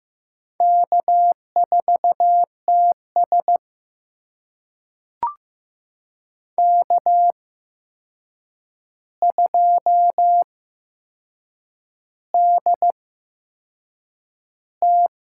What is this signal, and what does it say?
Morse code, heard as a keyed tone, about 15 wpm, spaced as for 6 wpm, 700 Hz. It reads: K4TS K2DT
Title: USA Call Signs - 4 Characters - 8x Character Spacing - 15wpm